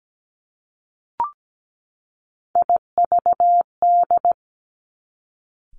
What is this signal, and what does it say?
Morse code, heard as a keyed tone, about 17 wpm, 700 Hz. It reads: IVD
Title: Three Challenging Letter-Number Combinations 17wpm